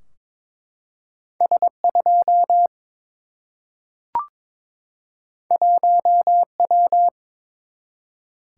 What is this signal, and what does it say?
Morse code, heard as a keyed tone, about 22 wpm, 700 Hz. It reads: S 2 1 W
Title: Single-Letter Single-Number 22wpm